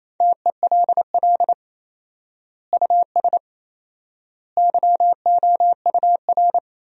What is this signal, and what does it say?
Morse code, heard as a keyed tone, about 28 wpm, 700 Hz. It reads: TELL US YOUR
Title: 4x Word-Spacing - Sentences from Top 200 Words - 28wpm